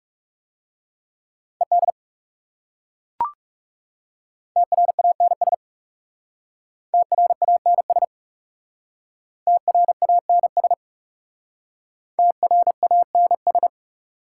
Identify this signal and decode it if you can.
Morse code, heard as a keyed tone, about 38 wpm, 690 Hz. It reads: ED TRANS TRANS TRANS TRANS